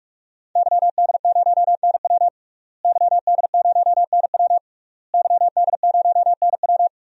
Morse code, heard as a keyed tone, about 46 wpm, 700 Hz. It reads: YD0NW YD0NW YD0NW
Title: DXCC-100 Call Signs - 5 Characters - Repeated 3x - 45wpm